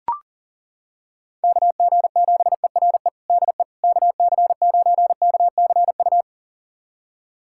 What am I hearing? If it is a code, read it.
Morse code, heard as a keyed tone, about 40 wpm, 700 Hz. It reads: KG7ERE DE KC9KCU
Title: USA Callsign DE Callsign 40wpm